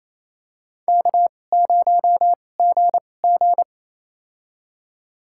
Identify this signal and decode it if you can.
Morse code, heard as a keyed tone, about 28 wpm, spaced as for 20 wpm, 700 Hz. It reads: K0GG
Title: USA Call Signs - 4 Characters - 2x Character Spacing - 28wpm